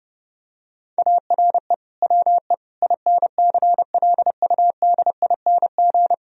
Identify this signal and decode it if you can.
Morse code, heard as a keyed tone, about 30 wpm, 700 Hz. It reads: ARE WE INCLUDING
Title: Sentences from Top 1000 Words - Review Part 4 - 30wpm